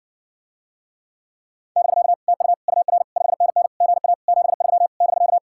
Morse code, heard as a keyed tone, about 59 wpm, 700 Hz. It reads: <BK> TU UR 5NN GA 73 <BK>